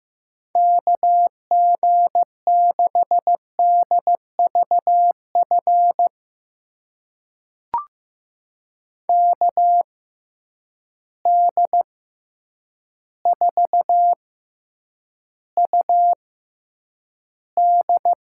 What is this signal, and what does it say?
Morse code, heard as a keyed tone, about 15 wpm, spaced as for 8 wpm, 700 Hz. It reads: KG6DVF KD4UD
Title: USA Call Signs - 6 Characters - 6x Character Spacing - 15wpm